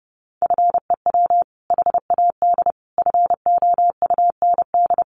Morse code, heard as a keyed tone, about 30 wpm, 700 Hz. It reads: FEW HAD FOUND
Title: Sentences from Top 300 Words 30wpm